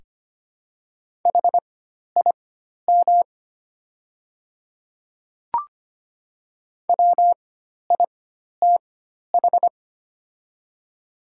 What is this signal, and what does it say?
Morse code, heard as a keyed tone, about 25 wpm, spaced as for 12 wpm, 700 Hz. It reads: HIM WITH
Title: 4x Character Spacing - Top 100 Words - 25wpm